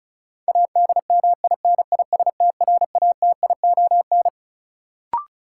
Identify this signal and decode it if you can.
Morse code, heard as a keyed tone, about 36 wpm, 700 Hz. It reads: ADMINISTRATION
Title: Sets of 5 Words - New Word Drill - 35wpm